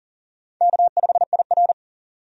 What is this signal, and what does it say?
Morse code, heard as a keyed tone, about 40 wpm, 700 Hz. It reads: K5IR
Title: USA Call Signs - 4 Characters - 40wpm